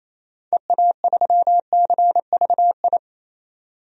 Morse code, heard as a keyed tone, about 27 wpm, 700 Hz. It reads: EA3CVI
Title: DXCC-100 Call Signs - 6 Characters - 28wpm